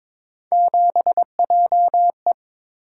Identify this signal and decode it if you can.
Morse code, heard as a keyed tone, about 22 wpm, 700 Hz. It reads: 7JE
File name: practice / 22wpm / Three Letter-Number Combinations - 22wpm